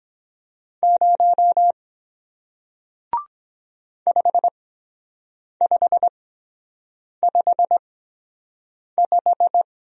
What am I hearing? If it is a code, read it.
Morse code, heard as a keyed tone, about 23 wpm, spaced as for 7 wpm, 700 Hz. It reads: 0 5555